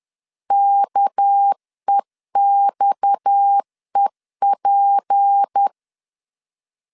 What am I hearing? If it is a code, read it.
Morse code, heard as a keyed tone, about 10 wpm, 800 Hz. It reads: KEXEP
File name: CW / DET / 1 6 1500C S B 10wpm 4137